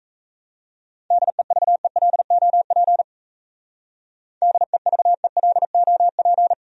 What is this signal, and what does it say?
Morse code, heard as a keyed tone, about 34 wpm, 700 Hz. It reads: DEVELOP DEVELOP
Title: Speed-Racing Top 500 Words 28wpm